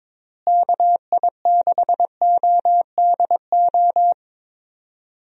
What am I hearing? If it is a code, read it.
Morse code, heard as a keyed tone, about 22 wpm, 700 Hz. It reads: KI6ODO
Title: USA Call Signs - 6 Characters - Repeated 3x - 22wpm